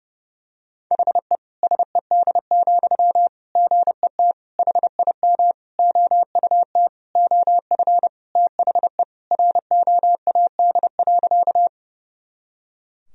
Morse code, heard as a keyed tone, about 30 wpm, 700 Hz. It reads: HE SED, GET HIM OUT OF THE ROAD.
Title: Sentences from Top 400 Words and Abbreviations 30wpm